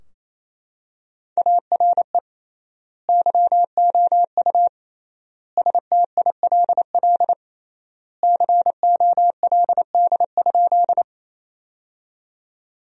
Morse code, heard as a keyed tone, about 28 wpm, 700 Hz. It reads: ARE YOU STILL COLD?
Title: 3x Word-Spacing - Sentences from Top 400 Words - 28wpm